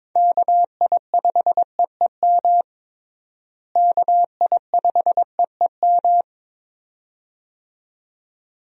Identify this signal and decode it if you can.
Morse code, heard as a keyed tone, about 22 wpm, 700 Hz. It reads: KI5EEM KI5EEM
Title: USA Call Signs - 6 Characters - Repeated 2x - 22wpm